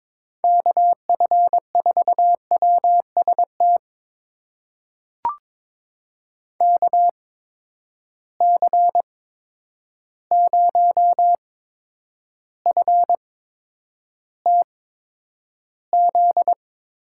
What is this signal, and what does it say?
Morse code, heard as a keyed tone, about 22 wpm, spaced as for 9 wpm, 700 Hz. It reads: KF4WST KC0FTZ
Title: USA Call Signs - 6 Characters - 8x Character Spacing - 22wpm